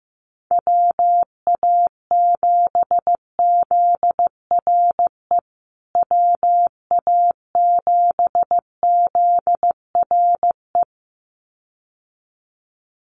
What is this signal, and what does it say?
Morse code, heard as a keyed tone, about 15 wpm, 700 Hz. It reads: WA7ZRE WA7ZRE
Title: USA Call Signs Repeated 2x 15wpm